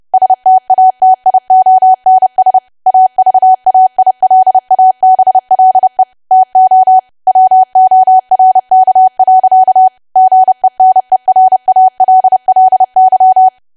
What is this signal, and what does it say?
Morse code, heard as a keyed tone, about 30 wpm, 750 Hz. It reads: STATIONS AVAILABLE TO WORK. GENERALLY
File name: cwa / W1AW-Level3 / 30WPM